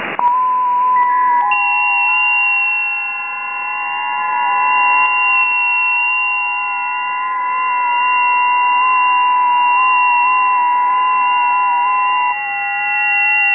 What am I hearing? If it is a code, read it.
FT8.